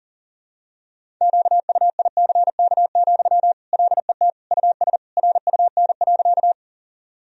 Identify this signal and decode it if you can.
Morse code, heard as a keyed tone, about 40 wpm, 700 Hz. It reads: QUICK, LET US RUN.
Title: Sentences from Top 1000 Words - Review Part 1 - 40wpm